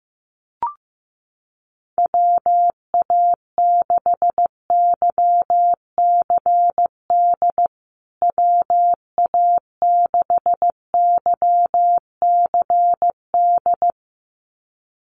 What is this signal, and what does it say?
Morse code, heard as a keyed tone, about 15 wpm, 700 Hz. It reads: WA6YCD WA6YCD
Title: USA Call Signs Repeated 2x 15wpm